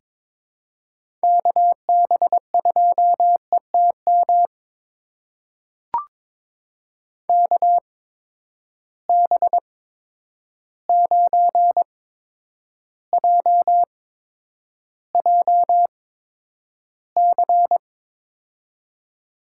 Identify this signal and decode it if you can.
Morse code, heard as a keyed tone, about 22 wpm, spaced as for 9 wpm, 700 Hz. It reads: KB2ETM KB9JJC
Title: USA Call Signs - 6 Characters - 8x Character Spacing - 22wpm